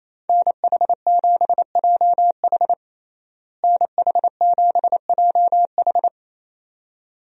Morse code, heard as a keyed tone, about 28 wpm, 700 Hz. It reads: NH7JH NH7JH
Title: USA Call Signs - 5 Characters - Repeated 2x - 28wpm